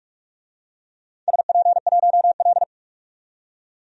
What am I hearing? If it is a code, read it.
Morse code, heard as a keyed tone, about 44 wpm, 700 Hz. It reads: SP1L
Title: DXCC-100 Call Signs - 4 Characters - 45wpm